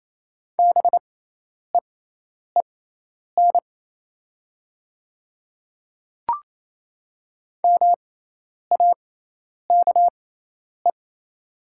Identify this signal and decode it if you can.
Morse code, heard as a keyed tone, about 28 wpm, spaced as for 10 wpm, 690 Hz. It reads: BEEN MAKE
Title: 6x Character Spacing - Top 100 Words - 28wpm